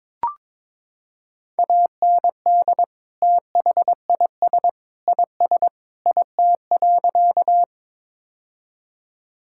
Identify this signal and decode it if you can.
Morse code, heard as a keyed tone, about 22 wpm, 700 Hz. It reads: AND THIS IS IT.
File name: practice / 22wpm / Sentences with 3-Word Phrases - 22wpm